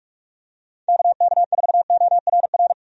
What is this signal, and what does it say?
Morse code, heard as a keyed tone, about 46 wpm, 700 Hz. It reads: KK4ORR
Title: USA Call Signs - 6 Characters - 45wpm